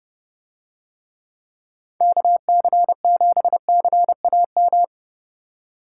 Morse code, heard as a keyed tone, about 28 wpm, 700 Hz. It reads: KC7CAM